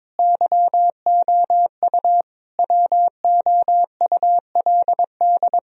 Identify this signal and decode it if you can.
Morse code, heard as a keyed tone, about 21 wpm, 700 Hz. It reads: YOU WOULD
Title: Sentences from Top 1000 Words - Review Part 2 - 22wpm